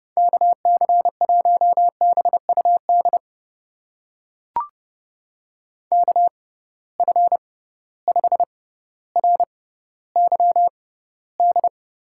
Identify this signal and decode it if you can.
Morse code, heard as a keyed tone, about 30 wpm, spaced as for 15 wpm, 700 Hz. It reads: KC1BUD KF5RYD
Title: USA Call Signs - 6 Characters - 6x Character Spacing - 30wpm